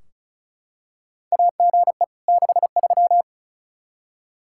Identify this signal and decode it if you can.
Morse code, heard as a keyed tone, about 36 wpm, 700 Hz. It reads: AGE 63
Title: Practice QSO - Exchange 3 - More Info - 35wpm